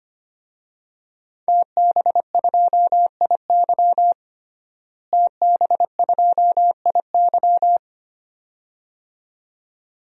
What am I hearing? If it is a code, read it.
Morse code, heard as a keyed tone, about 25 wpm, 700 Hz. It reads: TB2IY TB2IY
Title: DXCC-100 Call Signs - 5 Characters - Repeated 2x - 25wpm